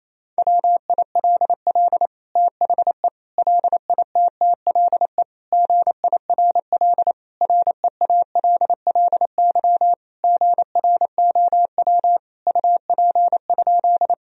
Morse code, heard as a keyed tone, about 28 wpm, 700 Hz. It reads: WILL THE LITTLE GIRL REALLY GROW UP?
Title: Sentences from Top 1000 Words - Review Part 1 - 28wpm